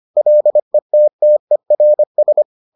Morse code, heard as a keyed tone, about 25 wpm, 600 Hz. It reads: LETTERS